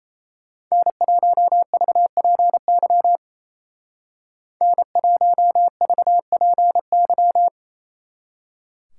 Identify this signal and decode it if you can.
Morse code, heard as a keyed tone, about 34 wpm, 700 Hz. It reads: N1VPY N1VPY
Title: Speed Racing - USA 1x3 Call Signs 28wpm